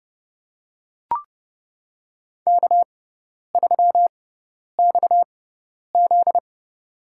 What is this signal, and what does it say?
Morse code, heard as a keyed tone, about 31 wpm, spaced as for 10 wpm, 700 Hz. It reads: K3XZ
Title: USA Call Signs - 4 Characters - 6x Character Spacing - 30wpm